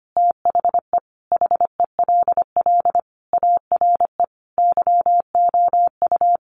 Morse code, heard as a keyed tone, about 25 wpm, 700 Hz. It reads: THE HELL ARE YOU
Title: Sets of 5 Words Encore 25wpm